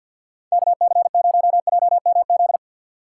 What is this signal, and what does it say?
Morse code, heard as a keyed tone, about 53 wpm, 700 Hz. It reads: KC0JMZ
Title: USA Call Signs - 3x Character Spacing - 50wpm